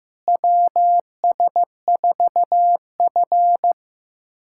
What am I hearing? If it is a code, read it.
Morse code, heard as a keyed tone, about 15 wpm, 700 Hz. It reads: WS4F
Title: USA Callsign DE Callsign - 4 Characters - 15wpm